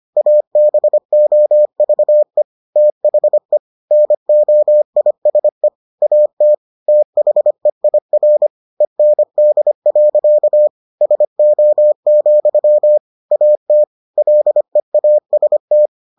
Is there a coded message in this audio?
Yes: Morse code from 0.2 to 15.9 s, heard as a keyed tone, about 25 wpm, 600 Hz. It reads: ABOVE THE NOISE AT THEIR END. SO, AT LEAST